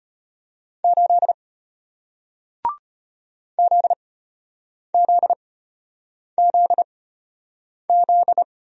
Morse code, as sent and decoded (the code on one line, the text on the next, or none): ---.. / --.. --.. --.. --..
8 ZZZZ